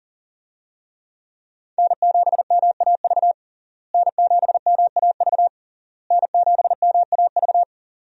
Morse code, heard as a keyed tone, about 40 wpm, 700 Hz. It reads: N7MAV N7MAV N7MAV
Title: USA Call Signs - 5 Characters - Repeated 3x - 40wpm